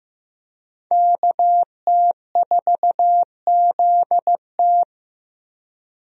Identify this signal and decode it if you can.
Morse code, heard as a keyed tone, about 15 wpm, 700 Hz. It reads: KT4ZT
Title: USA Call Signs - 5 Characters - 15wpm